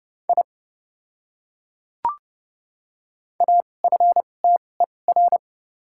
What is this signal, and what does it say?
Morse code, heard as a keyed tone, about 30 wpm, spaced as for 22 wpm, 700 Hz. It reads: I AFTER
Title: 2x Character Spacing - Top 300 Words - 30wpm